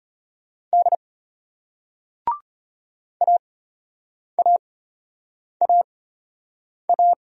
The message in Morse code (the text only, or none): D AAAA